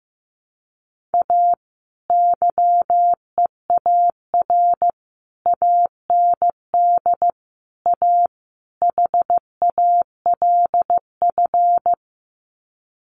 Morse code, heard as a keyed tone, about 15 wpm, 700 Hz. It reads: A YEAR AND A HALF
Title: Sets of 5 Words 15wpm